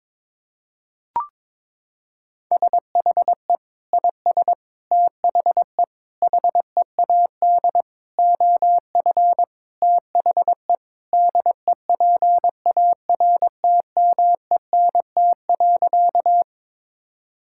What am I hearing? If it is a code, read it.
Morse code, heard as a keyed tone, about 22 wpm, 700 Hz. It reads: SHE IS THE HEAD OF THE DEPARTMENT.
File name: practice / 22wpm / Sentences with 3-Word Phrases - 22wpm